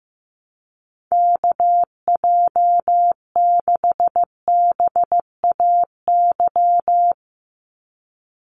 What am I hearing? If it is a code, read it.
Morse code, heard as a keyed tone, about 15 wpm, 700 Hz. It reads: KJ6BAY
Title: USA Call Signs 15wpm